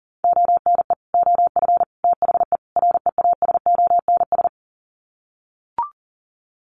Morse code, heard as a keyed tone, about 40 wpm, 700 Hz. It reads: ONE OF THE REASONS
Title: Sets of 4 Words 40wpm